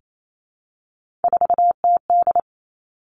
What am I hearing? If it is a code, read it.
Morse code, heard as a keyed tone, about 28 wpm, 700 Hz. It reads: 4TD
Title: Three Challenging Letter-Number Combinations 28wpm